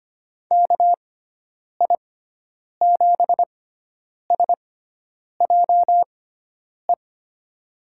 Morse code, heard as a keyed tone, about 25 wpm, spaced as for 9 wpm, 700 Hz. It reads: KI7SJE
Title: USA Call Signs - 6 Characters - 6x Character Spacing - 25wpm